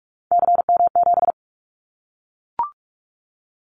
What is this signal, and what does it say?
Morse code, heard as a keyed tone, about 44 wpm, 700 Hz. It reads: CM7
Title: Callsign Prefixes for Top 100 DXCC - 45wpm